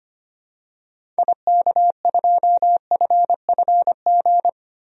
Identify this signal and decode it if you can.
Morse code, heard as a keyed tone, about 25 wpm, 700 Hz. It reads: IK2FFG
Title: DXCC-100 Call Signs - 6 Characters - Repeated 2x - 25wpm